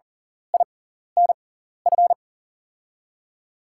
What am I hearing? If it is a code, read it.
Morse code, heard as a keyed tone, about 40 wpm, spaced as for 14 wpm, 700 Hz. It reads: INF
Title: USA Call Signs - 6x Character Spacing - 40wpm